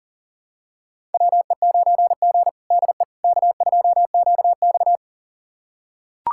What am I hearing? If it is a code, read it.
Morse code, heard as a keyed tone, about 40 wpm, 700 Hz. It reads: WE9G DE K2QX
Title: USA Callsign DE Callsign - 4 Characters - 40wpm